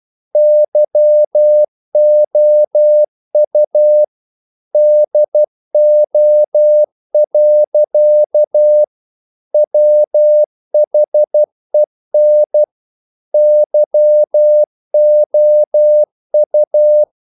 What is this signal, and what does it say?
Morse code, heard as a keyed tone, about 12 wpm, 600 Hz. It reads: YOU DO. WHEN YOU
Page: English quotations in CW (telegraphy - morse)